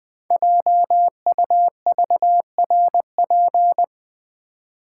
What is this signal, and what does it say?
Morse code, heard as a keyed tone, about 20 wpm, 700 Hz. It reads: JUVRP
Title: Code Groups - 5 Letters - 20wpm